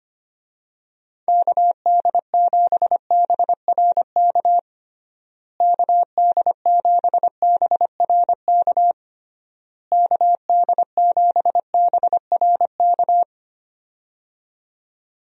Morse code, heard as a keyed tone, about 25 wpm, 700 Hz. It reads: KD7BRK KD7BRK KD7BRK
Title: USA Call Signs - 6 Characters - Repeated 3x - 25wpm